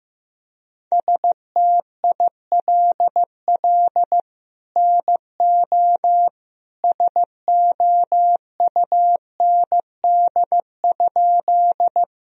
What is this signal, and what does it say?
Morse code, heard as a keyed tone, about 15 wpm, 700 Hz. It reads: STILL NO SOUND?
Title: Sentences from Top 1000 Words - Review Part 1 - 15wpm